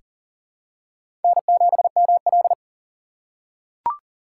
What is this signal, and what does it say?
Morse code, heard as a keyed tone, about 40 wpm, 700 Hz. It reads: N7ML
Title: K1USN SST QSO Practice - 40wpm